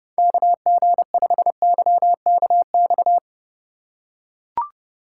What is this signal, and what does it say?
Morse code, heard as a keyed tone, about 29 wpm, 700 Hz. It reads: KG5YKX